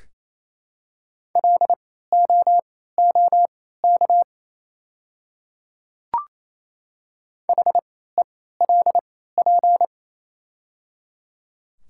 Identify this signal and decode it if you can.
Morse code, heard as a keyed tone, about 28 wpm, spaced as for 16 wpm, 690 Hz. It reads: LOOK HELP
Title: 3x Character Spacing - Top 200 Words - 28wpm